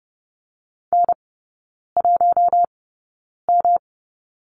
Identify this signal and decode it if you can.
Morse code, heard as a keyed tone, about 30 wpm, spaced as for 9 wpm, 700 Hz. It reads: N1M
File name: Lesson 18e - Letter m - Three Characters 30wpm